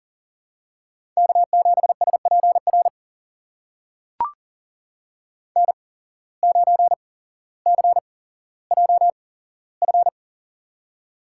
Morse code, heard as a keyed tone, about 39 wpm, spaced as for 17 wpm, 700 Hz. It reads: K7SPR N9CJF
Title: USA Call Signs - 5 Characters - 8x Character Spacing - 40wpm